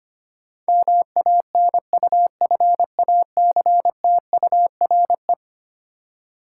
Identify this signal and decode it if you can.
Morse code, heard as a keyed tone, about 25 wpm, 700 Hz. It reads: MANUFACTURE
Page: morse-code-ninja-practice-25wpm